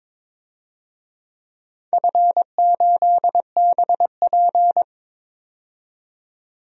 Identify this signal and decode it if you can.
Morse code, heard as a keyed tone, about 21 wpm, 700 Hz. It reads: F8BP
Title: DXCC-100 Call Signs - 4 Characters - 22wpm